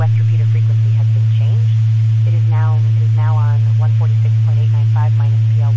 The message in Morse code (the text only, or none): none